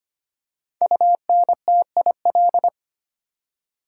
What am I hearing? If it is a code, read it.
Morse code, heard as a keyed tone, about 25 wpm, 700 Hz. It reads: UNTIL
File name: 5 Letter Words - First 3 Letters Given - 25wpm